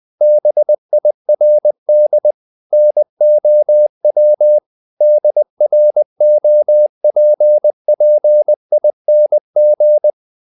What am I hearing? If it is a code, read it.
Morse code, heard as a keyed tone, about 20 wpm, 600 Hz. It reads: BIRD NOW DROPPING
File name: the old man and the sea / the old man and the sea part 1 20wpm